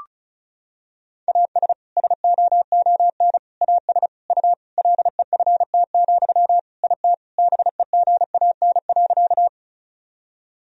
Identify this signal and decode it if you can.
Morse code, heard as a keyed tone, about 36 wpm, 700 Hz. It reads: AS SOON AS U LEFT, IT BEGAN.
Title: Sentences from Top 400 Words and Abbreviations 35wpm